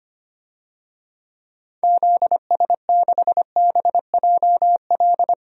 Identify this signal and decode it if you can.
Morse code, heard as a keyed tone, about 25 wpm, 700 Hz. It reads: ZS6BJL
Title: DXCC-100 Call Signs - 6 Characters - 25wpm